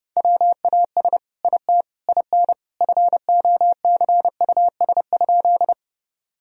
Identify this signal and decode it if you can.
Morse code, heard as a keyed tone, about 30 wpm, 700 Hz. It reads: WAS IT IN FOCUS?
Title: Sentences from Top 1000 Words - Review Part 4 - 30wpm